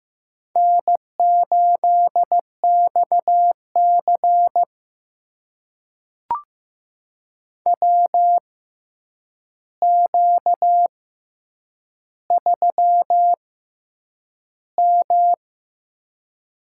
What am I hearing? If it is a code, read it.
Morse code, heard as a keyed tone, about 15 wpm, spaced as for 8 wpm, 700 Hz. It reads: N8XC WQ3M
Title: USA Call Signs - 4 Characters - 6x Character Spacing - 15wpm